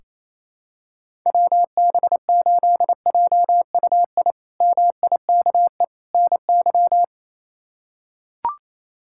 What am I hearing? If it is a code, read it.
Morse code, heard as a keyed tone, about 27 wpm, 700 Hz. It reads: WB8JUI MIKE NY
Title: K1USN SST QSO Practice - 28wpm